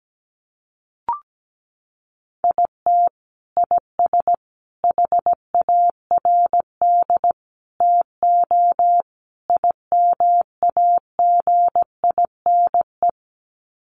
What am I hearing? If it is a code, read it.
Morse code, heard as a keyed tone, about 17 wpm, 700 Hz. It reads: IT IS HARD TO IMAGINE
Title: Sets of 5 Words Encore2 17wpm